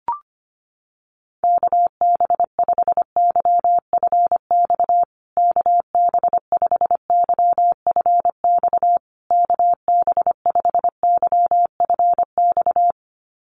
Morse code, heard as a keyed tone, about 25 wpm, 700 Hz. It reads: KB5YFX KB5YFX KB5YFX
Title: USA Call Signs Repeated 3x 25wpm